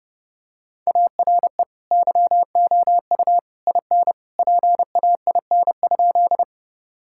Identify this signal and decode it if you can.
Morse code, heard as a keyed tone, about 30 wpm, 700 Hz. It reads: ARE YOU IN PAIN?
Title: Sentences from Top 1600 Words 30wpm